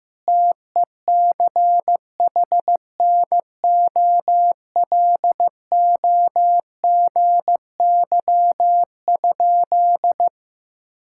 Morse code, heard as a keyed tone, about 15 wpm, 700 Hz. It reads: TECHNOLOGY?
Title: Sentences from Top 1000 Words - Review Part 5 - 15wpm